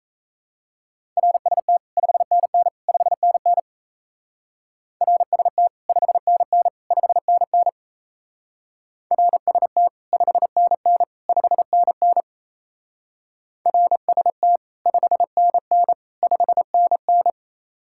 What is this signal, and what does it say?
Morse code, heard as a keyed tone, about 36 wpm, 700 Hz. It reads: RST 5NN 5NN RST 5NN 5NN RST 5NN 5NN RST 5NN 5NN